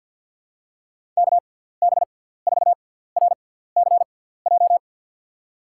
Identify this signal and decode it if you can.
Morse code, heard as a keyed tone, about 52 wpm, spaced as for 17 wpm, 700 Hz. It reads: KB4RCJ
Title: USA Call Signs - 6 Characters - 6x Character Spacing - 50wpm